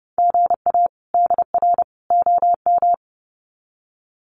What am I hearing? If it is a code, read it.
Morse code, heard as a keyed tone, about 30 wpm, 700 Hz. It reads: GA DR OM
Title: Common QSO Phrases 30wpm